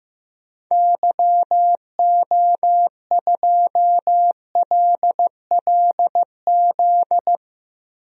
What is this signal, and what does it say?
Morse code, heard as a keyed tone, about 15 wpm, 700 Hz. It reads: YO2LLZ
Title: DXCC-100 Call Signs - 6 Characters - Repeated 2x - 15wpm